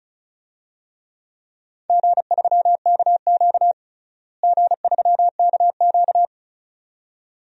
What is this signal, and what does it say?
Morse code, heard as a keyed tone, about 36 wpm, 700 Hz. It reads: G3KQ G3KQ
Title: DXCC-100 Call Signs - 4 Characters - Repeated 2x - 35wpm